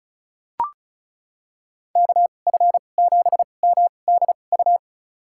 Morse code, heard as a keyed tone, about 34 wpm, spaced as for 25 wpm, 700 Hz. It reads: KF7MDU
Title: USA Call Signs - 6 Characters - 2x Character Spacing - 35wpm